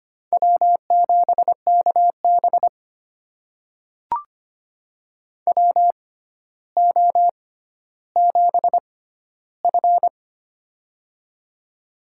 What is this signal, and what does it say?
Morse code, heard as a keyed tone, about 25 wpm, spaced as for 13 wpm, 700 Hz. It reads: W7KB WO7F
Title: USA Call Signs - 4 Characters - 6x Character Spacing - 25wpm